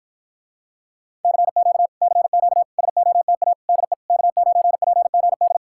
Morse code, heard as a keyed tone, about 53 wpm, 700 Hz. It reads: CQ CQ SOTA DE K9PGD